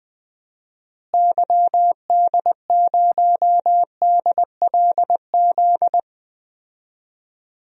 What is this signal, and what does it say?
Morse code, heard as a keyed tone, about 20 wpm, 700 Hz. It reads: YD0DLZ
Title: DXCC-100 Call Signs - 6 Characters - Repeated 3x - 20wpm